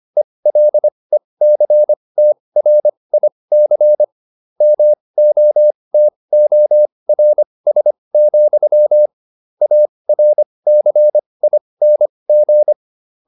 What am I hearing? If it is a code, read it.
Morse code, heard as a keyed tone, about 25 wpm, spaced as for 20 wpm, 600 Hz. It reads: ELECTRIC MOTORS, ARCING